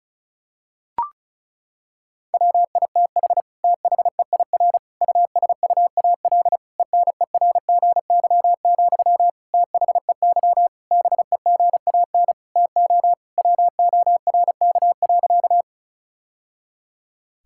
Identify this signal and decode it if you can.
Morse code, heard as a keyed tone, about 36 wpm, 700 Hz. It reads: WITH THEIR USUAL ENERGY, THEY BEGAN TO WORK.
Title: Sentences from Top 1000 Words - Review Part 6 - 35wpm